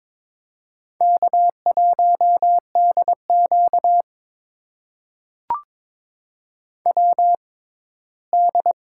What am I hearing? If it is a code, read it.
Morse code, heard as a keyed tone, about 22 wpm, spaced as for 15 wpm, 700 Hz. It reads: K1DQ WD
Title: USA Call Signs - 4 Characters - 6x Character Spacing - 22wpm